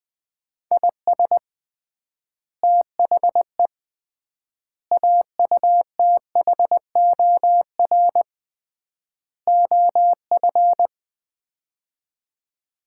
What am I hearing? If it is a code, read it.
Morse code, heard as a keyed tone, about 20 wpm, 700 Hz. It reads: IS THE AUTHOR OF
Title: 3x Word-Spacing - Sets of 4 Words - 20wpm